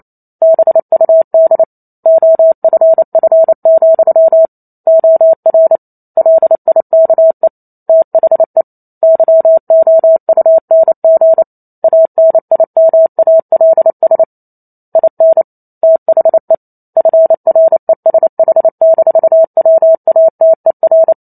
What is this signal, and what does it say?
Morse code, heard as a keyed tone, about 28 wpm, 650 Hz. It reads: BUD OFF, OR LIKE THE YOUNG ANIMALS IN THE FRESH-WATER